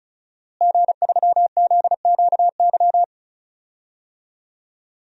Morse code, heard as a keyed tone, about 34 wpm, 700 Hz. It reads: G3ZQY
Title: DXCC-100 Call Signs - 5 Characters - Repeated 3x - 35wpm